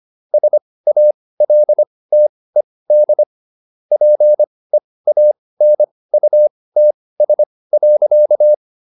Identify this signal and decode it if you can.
Morse code, heard as a keyed tone, about 25 wpm, spaced as for 18 wpm, 600 Hz. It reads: SALTED PEANUTS.